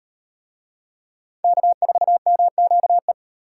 Morse code, heard as a keyed tone, about 38 wpm, 700 Hz. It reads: K4MQE